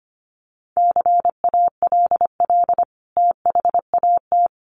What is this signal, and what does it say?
Morse code, heard as a keyed tone, about 25 wpm, 700 Hz. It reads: CALL THAT
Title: Sentences from Top 1100 Words 25wpm